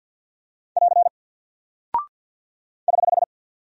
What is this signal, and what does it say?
Morse code, heard as a keyed tone, about 48 wpm, 700 Hz. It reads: <AR> <HH>
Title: Single Character (Letter:Number:Punctuation:Prosign) Alt 50wpm